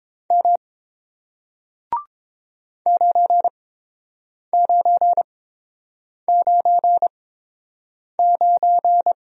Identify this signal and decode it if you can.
Morse code, heard as a keyed tone, about 30 wpm, spaced as for 7 wpm, 690 Hz. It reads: M 9999